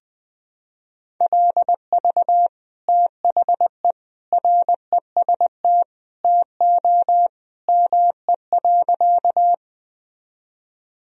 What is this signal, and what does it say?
Morse code, heard as a keyed tone, about 20 wpm, 700 Hz. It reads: LV THE REST TO ME.
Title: Sentences from Top 500 Words and Abbreviations 20wpm